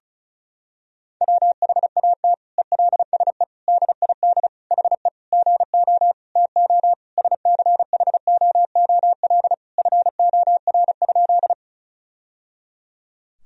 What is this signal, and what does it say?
Morse code, heard as a keyed tone, about 36 wpm, 700 Hz. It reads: WHAT ELSE DID HE GO TO SCHOOL FOR?
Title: Sentences from Top 1000 Words - Review Part 1 - 35wpm